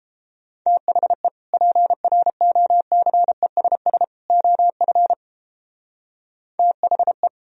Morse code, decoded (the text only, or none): THE PROCESS OF THE